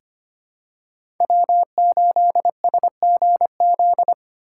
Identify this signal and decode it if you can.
Morse code, heard as a keyed tone, about 25 wpm, 700 Hz. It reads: W8SGZ
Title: USA Call Signs - 4x Character Spacing - 25wpm